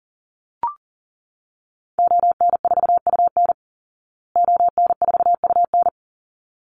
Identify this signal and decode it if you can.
Morse code, heard as a keyed tone, about 40 wpm, 700 Hz. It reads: ON4UN ON4UN
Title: Call Signs Repeated 3x - 40wpm